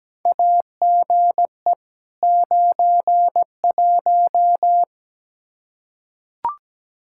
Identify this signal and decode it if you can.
Morse code, heard as a keyed tone, about 17 wpm, 700 Hz. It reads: AGE 91
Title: Practice QSO - Exchange 3 - More Info - 17wpm